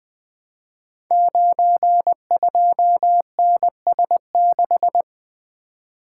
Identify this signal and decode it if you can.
Morse code, heard as a keyed tone, about 20 wpm, 700 Hz. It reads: 92NS6
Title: Code Groups - 5 Letter:Numbers - 20wpm